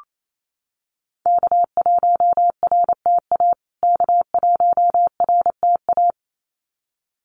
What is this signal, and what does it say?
Morse code, heard as a keyed tone, about 28 wpm, 700 Hz. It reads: K1RTA K1RTA
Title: USA Call Signs Repeated 2x 28wpm